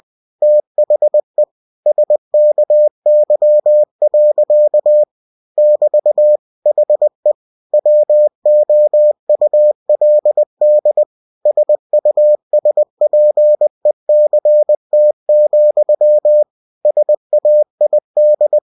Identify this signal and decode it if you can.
Morse code, heard as a keyed tone, about 20 wpm, 600 Hz. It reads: THE SKY. <BT> HE WOULD SUSPECT, SAID
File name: cw-books / winnie-the-pooh / winnie-20wpm-page-0036